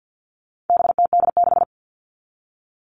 Morse code, heard as a keyed tone, about 50 wpm, 700 Hz. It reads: BTD6
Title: All Cut Numbers - 4-Digit 50wpm